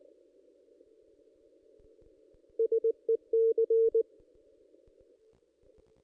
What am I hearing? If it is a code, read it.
Morse code, heard as a keyed tone, about 19 wpm, 440 Hz. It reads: SEC
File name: contests / ss-cw-k5tr-op-n5rz -1600